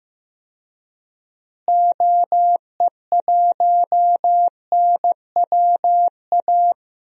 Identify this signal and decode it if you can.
Morse code, heard as a keyed tone, about 15 wpm, 700 Hz. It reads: OE1NWA
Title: DXCC-100 Call Signs - 6 Characters - Repeated 2x - 15wpm